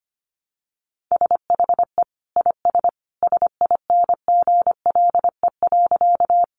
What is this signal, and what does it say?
Morse code, heard as a keyed tone, about 25 wpm, 700 Hz. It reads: SHE IS SINGLE.
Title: Sentences from Top 700 Words 25wpm